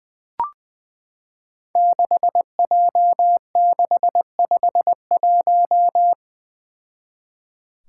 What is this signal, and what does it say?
Morse code, heard as a keyed tone, about 20 wpm, 700 Hz. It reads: 6J651
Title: Five Challenging Letter-Number Combinations 20wpm